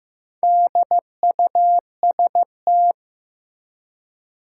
Morse code, heard as a keyed tone, about 15 wpm, 700 Hz. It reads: DUST
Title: Four Letter Words - 15wpm